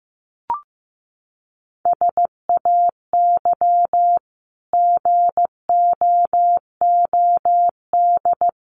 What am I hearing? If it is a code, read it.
Morse code, heard as a keyed tone, about 15 wpm, 700 Hz. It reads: SAY GOOD
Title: Sentences from Top 300 Words 15wpm